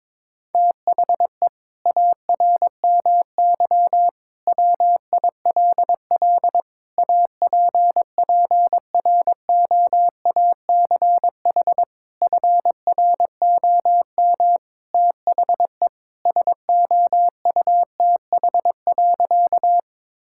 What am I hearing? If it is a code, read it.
Morse code, heard as a keyed tone, about 22 wpm, 700 Hz. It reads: THE ARMY WILL APPROACH FROM THE SOUTH.
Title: Sentences from Top 1000 Words - Review Part 6 - 22wpm